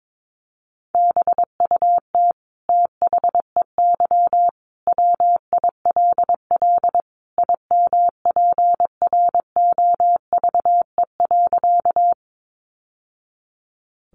Morse code, heard as a keyed tone, about 22 wpm, 700 Hz. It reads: BUT THEY WILL IMPROVE.
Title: Sentences from Top 800 Words 22wpm